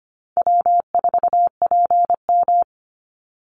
Morse code, heard as a keyed tone, about 25 wpm, 700 Hz. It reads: W4PM
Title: Call Signs - 25wpm